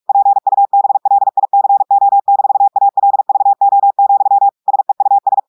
Morse code, heard as a keyed tone, about 46 wpm, 800 Hz. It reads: PUBLICO-ALVO, SEUS
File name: audios / portugais / portugais texte en clair 2 vitesse 45 mots minutes